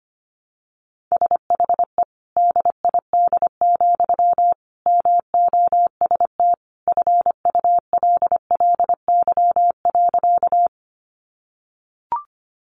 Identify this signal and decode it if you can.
Morse code, heard as a keyed tone, about 25 wpm, 700 Hz. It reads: SHE DID, MOST FULLY.
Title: Sentences from Top 1300 Words 25wpm